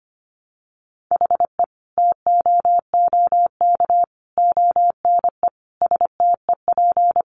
Morse code, heard as a keyed tone, about 25 wpm, 700 Hz. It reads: HE TOOK ONE STEP
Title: Sentences from Top 1300 Words 25wpm